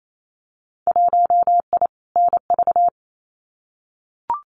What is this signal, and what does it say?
Morse code, heard as a keyed tone, about 28 wpm, 700 Hz. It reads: 1I NV
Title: Winter Field Day Exchanges Encore 28wpm